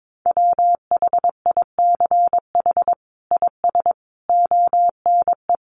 Morse code, heard as a keyed tone, about 22 wpm, 700 Hz. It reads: WHICH IS ONE
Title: Sets of 5 Words Encore 22wpm